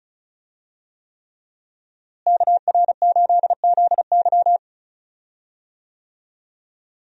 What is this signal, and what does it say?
Morse code, heard as a keyed tone, about 34 wpm, 700 Hz. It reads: KR8ZY
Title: USA Call Signs - 5 Characters - Repeated 3x - 35wpm